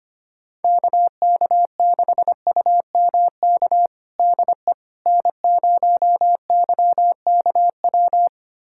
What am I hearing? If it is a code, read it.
Morse code, heard as a keyed tone, about 25 wpm, 700 Hz. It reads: KK6UMK DE N0YKW
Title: USA Callsign DE Callsign 25wpm